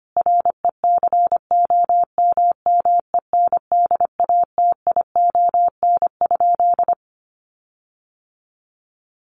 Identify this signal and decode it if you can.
Morse code, heard as a keyed tone, about 25 wpm, 700 Hz. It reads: RECOMMENDATION?
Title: Sentences from Top 1400 Words 25wpm